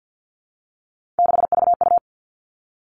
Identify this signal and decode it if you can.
Morse code, heard as a keyed tone, about 48 wpm, 700 Hz. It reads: 6VU
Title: All Cut Numbers - 3-Digit 50wpm